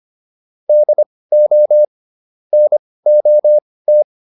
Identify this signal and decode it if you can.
Morse code, heard as a keyed tone, about 25 wpm, spaced as for 18 wpm, 600 Hz. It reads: DO NOT
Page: intermediate-practice-files